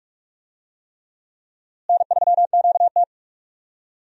Morse code, heard as a keyed tone, about 46 wpm, 700 Hz. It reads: N3QT
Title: USA Call Signs - 4 Characters - Repeated 3x - 45wpm